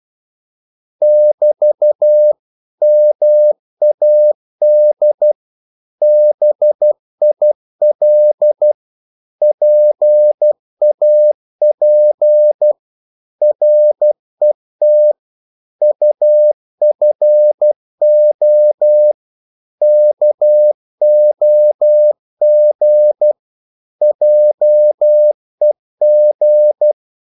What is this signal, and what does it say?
Morse code, heard as a keyed tone, about 12 wpm, 600 Hz. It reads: <BT> MAD BIL PAP RET UFO KOG JEG
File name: Korte ord DK 12wpm